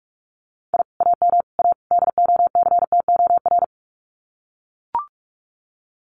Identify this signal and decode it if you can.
Morse code, heard as a keyed tone, about 44 wpm, 700 Hz. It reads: I AM A DOCTOR
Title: Common QSO Phrases Encore 45wpm